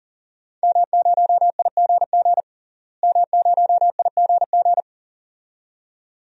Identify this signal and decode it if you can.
Morse code, heard as a keyed tone, about 40 wpm, 700 Hz. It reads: M0IGG M0IGG
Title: DXCC-100 Call Signs - 5 Characters - Repeated 2x - 40wpm